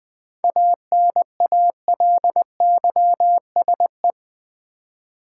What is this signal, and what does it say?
Morse code, heard as a keyed tone, about 20 wpm, 700 Hz. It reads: ANALYSE
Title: 7 Letter Words - First 3 Letters Given - 20wpm